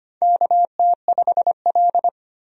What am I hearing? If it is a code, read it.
Morse code, heard as a keyed tone, about 25 wpm, 700 Hz. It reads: KT5L
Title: USA Call Signs - 4 Characters - 25wpm